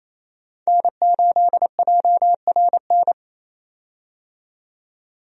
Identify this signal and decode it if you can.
Morse code, heard as a keyed tone, about 28 wpm, 700 Hz. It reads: N8JRN
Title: USA Call Signs - 5 Characters - 28wpm